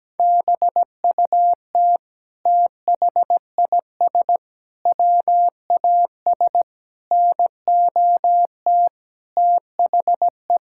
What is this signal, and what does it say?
Morse code, heard as a keyed tone, about 17 wpm, 700 Hz. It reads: BUT THIS WAS NOT THE
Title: Sentences from Top 1000 Words - Review Part 6 - 17wpm